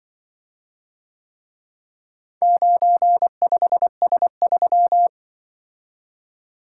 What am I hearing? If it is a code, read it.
Morse code, heard as a keyed tone, about 24 wpm, 700 Hz. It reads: 95S3
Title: Speed-Racing - Four Letter-Number Combinations - 20wpm